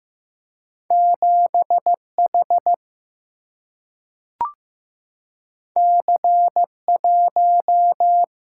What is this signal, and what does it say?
Morse code, heard as a keyed tone, about 15 wpm, 700 Hz. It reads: 7H C1
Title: Single-Letter Single-Number 15wpm